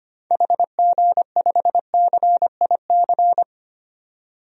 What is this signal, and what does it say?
Morse code, heard as a keyed tone, about 25 wpm, 700 Hz. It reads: HG5CIC
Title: DXCC-100 Call Signs - 6 Characters - 25wpm